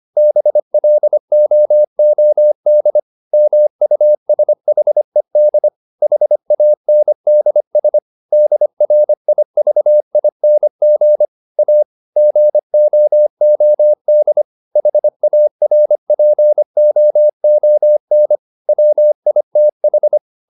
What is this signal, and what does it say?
Morse code, heard as a keyed tone, about 25 wpm, 600 Hz. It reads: BLOOD MUSHED HANDS DRIVING A GOOD HARPOON WITH